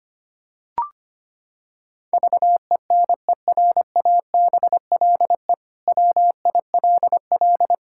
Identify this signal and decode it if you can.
Morse code, heard as a keyed tone, about 25 wpm, 700 Hz. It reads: VENERABLE WILL
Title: Pangram Challenge v2 - 25wpm